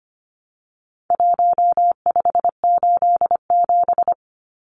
Morse code, heard as a keyed tone, about 25 wpm, 700 Hz. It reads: 1587
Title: Numbers - Contest Style - 4-Digit 25wpm